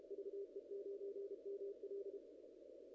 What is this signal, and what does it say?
Morse code, heard as a keyed tone, about 31 wpm, 410 Hz. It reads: VE9ML
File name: contests / ss-cw-k5tr-op-n5rz -1800